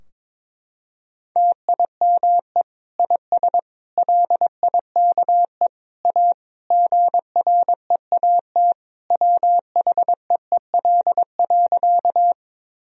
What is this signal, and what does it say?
Morse code, heard as a keyed tone, about 22 wpm, 700 Hz. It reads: TIME IS LIKE A GREAT WHEEL.
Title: Sentences from Top 1000 Words - Review Part 3 - 22wpm